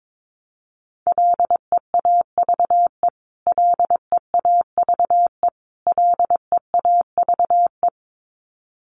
Morse code, heard as a keyed tone, about 22 wpm, 700 Hz. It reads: LEAVE LEAVE LEAVE
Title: Top 500 Words Repeated 22wpm